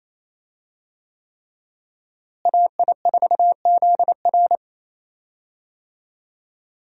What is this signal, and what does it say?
Morse code, heard as a keyed tone, about 28 wpm, 700 Hz. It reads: AI4ZR
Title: USA Call Signs - 5 Characters - Repeated 3x - 28wpm